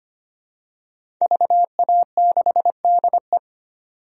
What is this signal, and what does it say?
Morse code, heard as a keyed tone, about 25 wpm, 700 Hz. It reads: VA6DE
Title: DXCC-100 Call Signs - 5 Characters - 25wpm